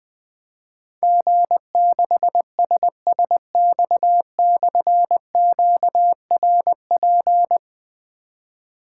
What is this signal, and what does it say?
Morse code, heard as a keyed tone, about 20 wpm, 700 Hz. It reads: G6SSX/QRP
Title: Difficult Call Signs 20wpm